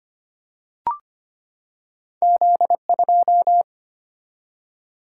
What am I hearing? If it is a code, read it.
Morse code, heard as a keyed tone, about 25 wpm, 700 Hz. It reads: Z2
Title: Single-Letter Single-Number 25wpm